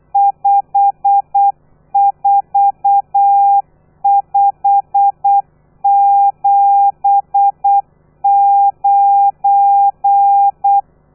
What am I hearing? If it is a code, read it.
Morse code, heard as a keyed tone, about 8 wpm, 800 Hz. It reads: 54579
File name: telegrafia / i2tfj / Lez12 35